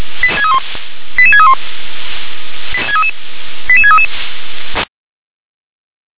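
Five-tone selective call (ZVEI), heard as quick runs of five tones, tones of about 80 ms. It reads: 88631 88633
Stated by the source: ZVEI five-tone selective call received on an SDR.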